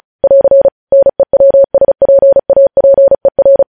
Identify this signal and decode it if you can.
Morse code, heard as a keyed tone, about 36 wpm, 550 Hz. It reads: <AR> NEWSPAPER
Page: news headlines in morse code at 35 wpm